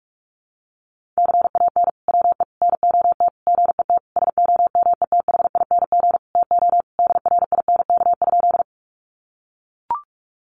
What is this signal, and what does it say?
Morse code, heard as a keyed tone, about 44 wpm, 700 Hz. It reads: CAN WE NOT GET SOMETHING TO DRINK?
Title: Sentences from Top 1500 Words 45wpm